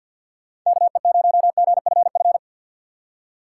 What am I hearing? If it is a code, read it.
Morse code, heard as a keyed tone, about 53 wpm, 700 Hz. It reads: KE0GFF
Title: USA Call Signs - 6 Characters - 4x Character Spacing - 50wpm